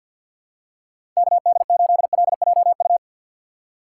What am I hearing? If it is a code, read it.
Morse code, heard as a keyed tone, about 52 wpm, 700 Hz. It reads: KD8LJU
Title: USA Call Signs - 6 Characters - 6x Character Spacing - 50wpm